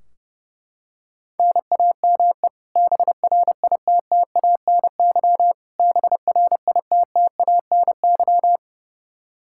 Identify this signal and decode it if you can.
Morse code, heard as a keyed tone, about 30 wpm, 700 Hz. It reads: NAME BRITTANY BRITTANY
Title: Practice QSO - Exchange 1 - Beginning - 30wpm